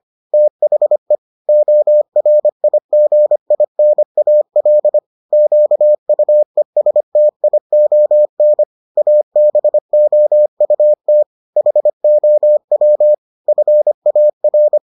Morse code, heard as a keyed tone, about 25 wpm, 600 Hz. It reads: THE ORIGINAL QUESTION ABOUT HOW FAR